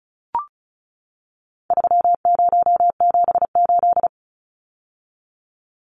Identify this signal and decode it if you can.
Morse code, heard as a keyed tone, about 34 wpm, 700 Hz. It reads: 3078
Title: Numbers - 4-Digit 35wpm